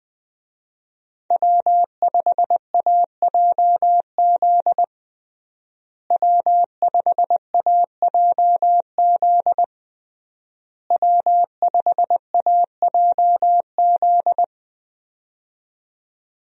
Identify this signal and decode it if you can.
Morse code, heard as a keyed tone, about 20 wpm, 700 Hz. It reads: W5AJZ W5AJZ W5AJZ
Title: USA Call Signs - 5 Characters - Repeated 3x - 20wpm